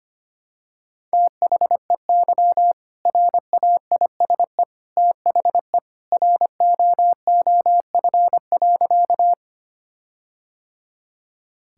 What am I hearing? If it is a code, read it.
Morse code, heard as a keyed tone, about 25 wpm, 700 Hz. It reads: THEY RAISE THE ROOF.
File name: practice / 25wpm / All-Age Appropriate Jokes - 25wpm